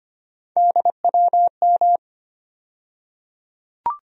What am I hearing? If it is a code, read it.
Morse code, heard as a keyed tone, about 25 wpm, 700 Hz. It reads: DWM
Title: USA Call Sign Postfixes - 3 Letters 25wpm